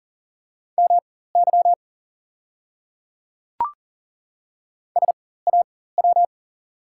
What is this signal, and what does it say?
Morse code, heard as a keyed tone, about 38 wpm, spaced as for 18 wpm, 690 Hz. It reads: MY SAW
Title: 4x Character Spacing - Top 300 Words - 40wpm